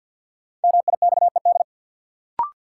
Morse code, heard as a keyed tone, about 51 wpm, 700 Hz. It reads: MIXED
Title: Sentences from Top 1500 Words - New Word Drill Alt - 50wpm